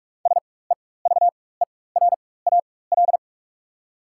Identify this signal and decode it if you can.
Morse code, heard as a keyed tone, about 42 wpm, spaced as for 21 wpm, 690 Hz. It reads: SEVERAL
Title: 4x Character Spacing - Top 500 Words - 45wpm